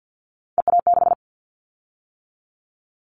Morse code, heard as a keyed tone, about 50 wpm, 700 Hz. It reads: EA6
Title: Callsign Prefixes for Top 50 DXCC - 50wpm